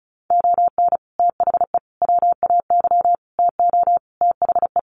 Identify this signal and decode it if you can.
Morse code, heard as a keyed tone, about 34 wpm, 700 Hz. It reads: ON THE WAY TO THE
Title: Sets of 5 Words 35wpm